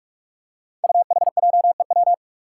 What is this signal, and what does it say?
Morse code, heard as a keyed tone, about 46 wpm, 700 Hz. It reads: UHJEW
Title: Five Challenging Letter-Number Combinations 45wpm